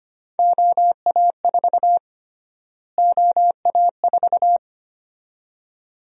Morse code, heard as a keyed tone, about 25 wpm, 700 Hz. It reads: OA4 OA4
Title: USA Call Sign Prefixes - Repeated 2x - 25wpm